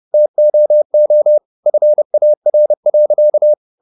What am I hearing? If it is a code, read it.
Morse code, heard as a keyed tone, about 30 wpm, 600 Hz. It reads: TOO FAR.
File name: the old man and the sea / the old man and the sea 1 30wpm